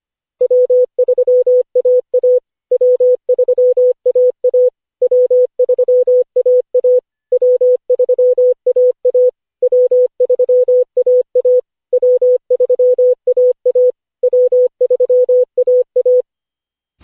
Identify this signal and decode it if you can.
Morse code, heard as a keyed tone, about 25 wpm, 500 Hz. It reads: W3AA W3AA W3AA W3AA W3AA W3AA W3AA
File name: cw / Session5 cw training